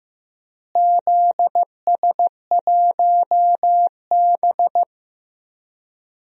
Morse code, heard as a keyed tone, about 15 wpm, 700 Hz. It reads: ZS1B
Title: DXCC-100 Call Signs - 4 Characters - Repeated 2x - 15wpm